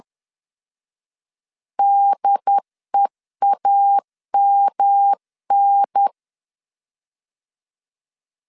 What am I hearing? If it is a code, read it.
Morse code, heard as a keyed tone, about 10 wpm, 800 Hz. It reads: DEAMN
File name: 10wpm